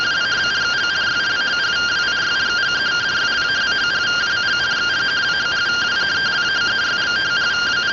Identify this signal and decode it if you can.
Two-tone teleprinter signal, about 170 Hz shift.